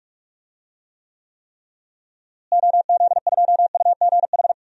Morse code, heard as a keyed tone, about 46 wpm, 700 Hz. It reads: OZ2UGH
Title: DXCC-100 Call Signs - 6 Characters - Repeated 3x - 45wpm